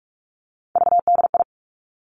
Morse code, heard as a keyed tone, about 46 wpm, 700 Hz. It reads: VDI
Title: Three Challenging Letter-Number Combinations 45wpm